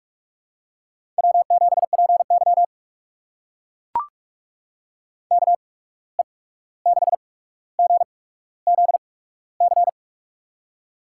Morse code, heard as a keyed tone, about 43 wpm, spaced as for 17 wpm, 700 Hz. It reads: W7PY KE6GZC
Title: USA Call Signs - 8x Character Spacing - 45wpm